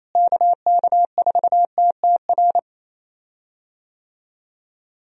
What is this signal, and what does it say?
Morse code, heard as a keyed tone, about 28 wpm, 700 Hz. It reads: KK4TTR
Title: USA Call Signs - 6 Characters - Repeated 2x - 28wpm